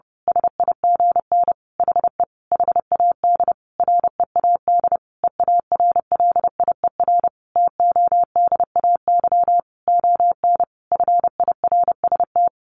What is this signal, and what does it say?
Morse code, heard as a keyed tone, about 30 wpm, 700 Hz. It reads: SIGN HE HAD READ EARLIER TODAY ON FIRST
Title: A Sound Of Thunder 30wpm